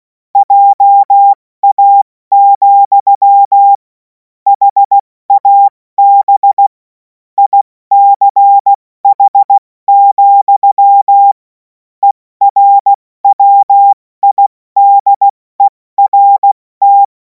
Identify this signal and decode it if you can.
Morse code, heard as a keyed tone, about 16 wpm, spaced as for 14 wpm, 800 Hz. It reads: JA, HAB ICH, ERWIDERT
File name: Morsekurs / Witze / Witz 07. April 2026 70 F80